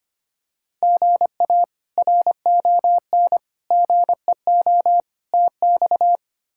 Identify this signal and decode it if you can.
Morse code, heard as a keyed tone, about 25 wpm, 700 Hz. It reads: GA RON GEO TX
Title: K1USN SST QSO Practice - 25wpm